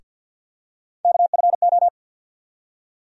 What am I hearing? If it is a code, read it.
Morse code, heard as a keyed tone, about 49 wpm, 700 Hz. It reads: KFO
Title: USA Call Sign Postfixes - 3 Letters 50wpm